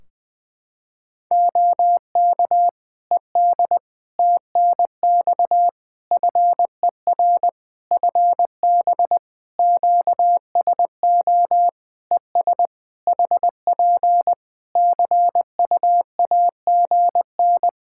Morse code, heard as a keyed tone, about 20 wpm, 700 Hz. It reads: OK ED TNX FER FB QSO ES HP CUAGN
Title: K1USN SST QSO Practice 20wpm